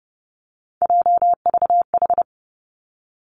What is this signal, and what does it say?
Morse code, heard as a keyed tone, about 31 wpm, 700 Hz. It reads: JVH
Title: Three Challenging Letter-Number Combinations 30wpm